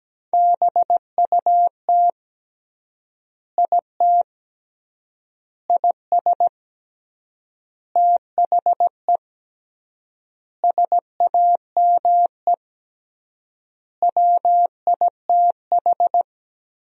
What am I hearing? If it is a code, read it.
Morse code, heard as a keyed tone, about 17 wpm, 700 Hz. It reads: BUT IT IS THE SAME WITH
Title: 3x Word-Spacing - Sentences from Top 300 Words - 17wpm